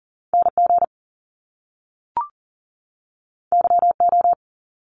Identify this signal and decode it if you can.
Morse code, heard as a keyed tone, about 40 wpm, 700 Hz. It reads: NG YO
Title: Top Two Letter Combinations in Words 40wpm